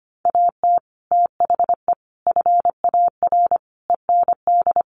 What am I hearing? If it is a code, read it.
Morse code, heard as a keyed tone, about 25 wpm, 700 Hz. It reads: AT THE FAR END
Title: Sets of 5 Words 25wpm